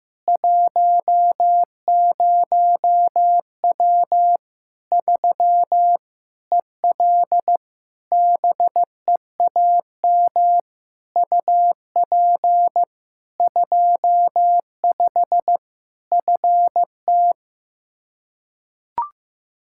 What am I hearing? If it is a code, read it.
Morse code, heard as a keyed tone, about 15 wpm, 700 Hz. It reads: 10W 3 EL BEAM UP 25 FT
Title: K1USN SST QSO Practice 15wpm